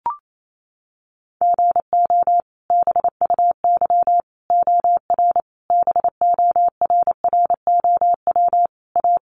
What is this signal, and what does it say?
Morse code, heard as a keyed tone, about 28 wpm, 700 Hz. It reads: GO BUY OR BORROW A
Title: Sentences from Top 1500 Words 28wpm